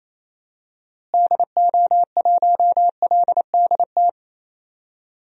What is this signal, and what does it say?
Morse code, heard as a keyed tone, about 28 wpm, 700 Hz. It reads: DO1LDT
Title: DXCC-100 Call Signs - 6 Characters - Repeated 2x - 28wpm